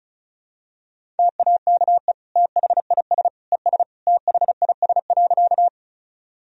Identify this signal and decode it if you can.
Morse code, heard as a keyed tone, about 34 wpm, 690 Hz. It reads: TAKE THIS ES THIS.
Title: Sentences from Top 200 Words and Abbreviations 35wpm